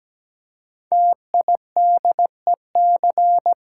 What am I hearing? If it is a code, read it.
Morse code, heard as a keyed tone, about 17 wpm, 700 Hz. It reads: TIDEC
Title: Code Groups - 5 Letters - 17wpm